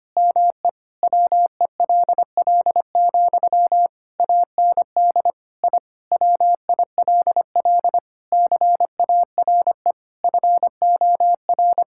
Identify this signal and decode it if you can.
Morse code, heard as a keyed tone, about 25 wpm, 700 Hz. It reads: ME WELL, AND I WILL CARE FOR
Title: Sentences from Top 1000 Words - Review Part 6 - 25wpm